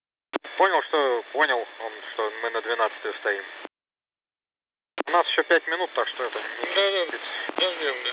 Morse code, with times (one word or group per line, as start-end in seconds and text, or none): none